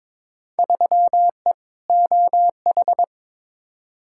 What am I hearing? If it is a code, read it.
Morse code, heard as a keyed tone, about 22 wpm, 700 Hz. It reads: 3E OH
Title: Field Day Exchanges - 22wpm